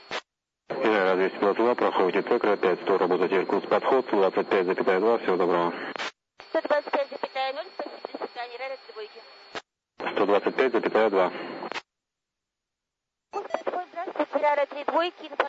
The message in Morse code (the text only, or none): none